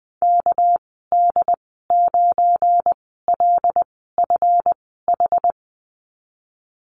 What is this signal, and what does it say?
Morse code, heard as a keyed tone, about 20 wpm, spaced as for 15 wpm, 700 Hz. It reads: KD9LFH
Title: 200 Lesson 32g - Letter k - Call Signs 20wpm